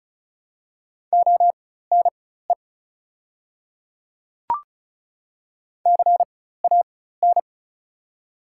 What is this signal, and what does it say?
Morse code, heard as a keyed tone, about 34 wpm, spaced as for 16 wpm, 700 Hz. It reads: ONE CAN
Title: 4x Character Spacing - Top 100 Words - 35wpm